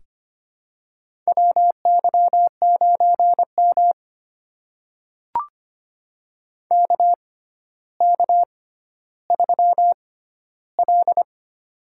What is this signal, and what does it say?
Morse code, heard as a keyed tone, about 25 wpm, spaced as for 13 wpm, 700 Hz. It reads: WY9M KK3L
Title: USA Call Signs - 4 Characters - 6x Character Spacing - 25wpm